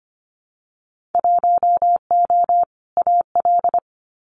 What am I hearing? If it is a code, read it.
Morse code, heard as a keyed tone, about 25 wpm, 700 Hz. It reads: 1O AL
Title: Winter Field Day Exchanges Encore 25wpm